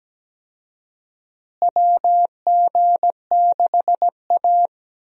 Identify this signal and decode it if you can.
Morse code, heard as a keyed tone, about 17 wpm, 700 Hz. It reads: WG6A
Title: USA Call Signs - 4 Characters - Repeated 2x - 17wpm